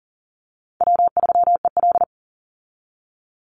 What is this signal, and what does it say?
Morse code, heard as a keyed tone, about 40 wpm, 700 Hz. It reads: W3EL
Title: Easy Call Signs - 40wpm